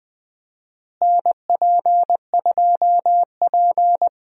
Morse code, heard as a keyed tone, about 21 wpm, 700 Hz. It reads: NP2P
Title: USA Call Signs - 4 Characters - Repeated 3x - 20wpm